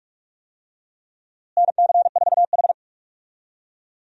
Morse code, heard as a keyed tone, about 41 wpm, 700 Hz. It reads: NC4H
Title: USA Call Signs - 4 Characters - Repeated 3x - 45wpm